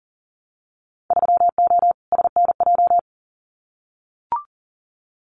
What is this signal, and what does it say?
Morse code, heard as a keyed tone, about 40 wpm, 700 Hz. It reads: 3O SNJ
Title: Winter Field Day Exchanges Encore 40wpm